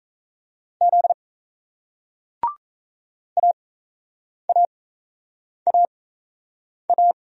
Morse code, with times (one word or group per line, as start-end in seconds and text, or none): 0.8-1.1 Z
3.4-7.1 AAAA